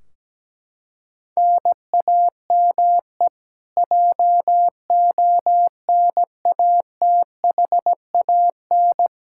Morse code, heard as a keyed tone, about 17 wpm, 700 Hz. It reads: NAME JONATHAN
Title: Practice QSO - Exchange 1 - Beginning - 17wpm